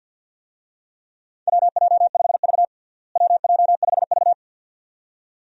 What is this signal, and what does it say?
Morse code, heard as a keyed tone, about 47 wpm, 700 Hz. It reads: WJ5V WJ5V
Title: USA Call Signs - 4 Characters - Repeated 2x - 50wpm